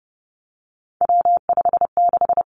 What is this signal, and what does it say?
Morse code, heard as a keyed tone, about 30 wpm, 700 Hz. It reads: W56
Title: Three Challenging Letter-Number Combinations 30wpm